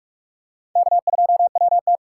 Morse code, heard as a keyed tone, about 44 wpm, 700 Hz. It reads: K2WT